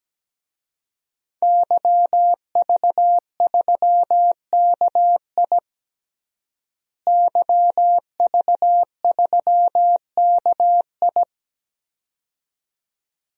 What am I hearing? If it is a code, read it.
Morse code, heard as a keyed tone, about 17 wpm, 700 Hz. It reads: YV3KI YV3KI
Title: DXCC-100 Call Signs - 5 Characters - Repeated 2x - 17wpm